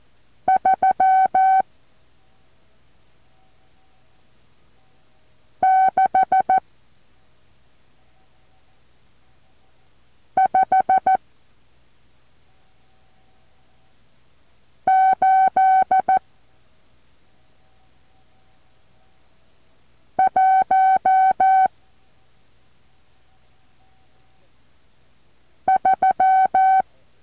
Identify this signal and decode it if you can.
Morse code, heard as a keyed tone, about 14 wpm, 760 Hz. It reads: 3 6 5 8 1 3
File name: intl morse code2